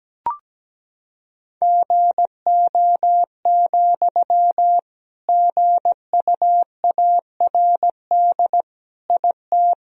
Morse code, heard as a keyed tone, about 17 wpm, 700 Hz. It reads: GO, GUARD IT
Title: Sentences from Top 1000 Words - Review Part 4 - 17wpm